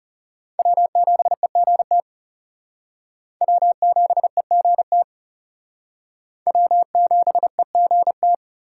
Morse code, heard as a keyed tone, about 35 wpm, 700 Hz. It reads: W7EGT W7EGT W7EGT
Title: Speed Racing - USA 1x3 Call Signs 30wpm